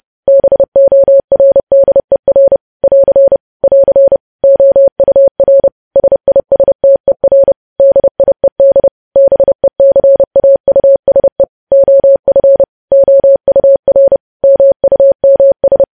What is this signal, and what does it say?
Morse code, heard as a keyed tone, about 30 wpm, 550 Hz. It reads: BORDER <AR> <AR> OUR SISTER DIED BECAUSE OF OUR MUMS
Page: news headlines in morse code at 30 wpm